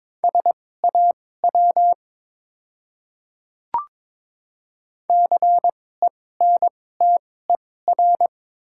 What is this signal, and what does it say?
Morse code, heard as a keyed tone, about 22 wpm, spaced as for 16 wpm, 700 Hz. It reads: SAW CENTER
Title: 2x Character Spacing - Top 500 Words - 22wpm